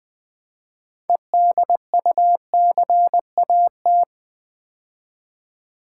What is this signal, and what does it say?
Morse code, heard as a keyed tone, about 20 wpm, 700 Hz. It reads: EDUCAT
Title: Build A Word - 2 Letters At A Time - 8-Letter Words - 20wpm